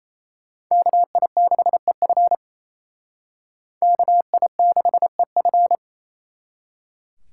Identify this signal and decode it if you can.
Morse code, heard as a keyed tone, about 37 wpm, 700 Hz. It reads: KI6EF KI6EF